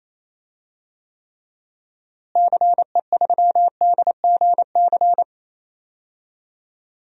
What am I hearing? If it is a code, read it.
Morse code, heard as a keyed tone, about 28 wpm, 700 Hz. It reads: CE3DGC
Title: DXCC-100 Call Signs - 6 Characters - Repeated 2x - 28wpm